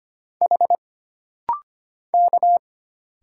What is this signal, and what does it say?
Morse code, heard as a keyed tone, about 25 wpm, 700 Hz. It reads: H K
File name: Single Character (Letter:Number:Punctuation:Prosign) - Rapid-Fire - 25wpm